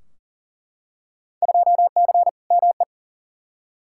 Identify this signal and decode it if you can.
Morse code, heard as a keyed tone, about 40 wpm, 700 Hz. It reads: 2C ME
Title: Field Day Exchanges Encore - 40wpm